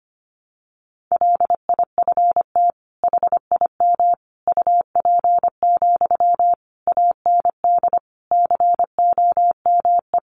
Morse code, heard as a keyed tone, about 25 wpm, 700 Hz. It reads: LIFT HIM UP, AND COME
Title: Sentences from Top 1000 Words 25wpm